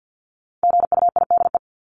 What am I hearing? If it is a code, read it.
Morse code, heard as a keyed tone, about 50 wpm, 700 Hz. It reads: GUIDE
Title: Top 501-750 Words 50wpm